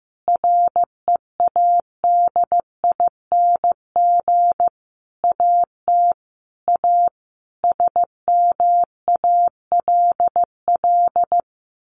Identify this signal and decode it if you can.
Morse code, heard as a keyed tone, about 15 wpm, 700 Hz. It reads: READING AT A SMALL
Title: Sentences from Top 900 Words 15wpm